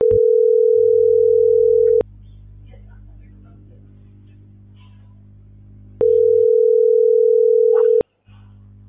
Ringback tone (North America), 440 + 480 Hz.